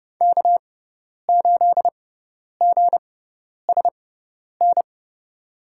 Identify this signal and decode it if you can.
Morse code, heard as a keyed tone, about 30 wpm, spaced as for 10 wpm, 700 Hz. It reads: K8GSN
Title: USA Call Signs - 5 Characters - 6x Character Spacing - 30wpm